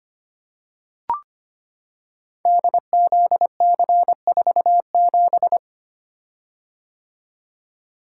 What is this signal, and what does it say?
Morse code, heard as a keyed tone, about 25 wpm, 700 Hz. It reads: DZC47
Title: Code Groups - 5 Letter:Numbers - 25wpm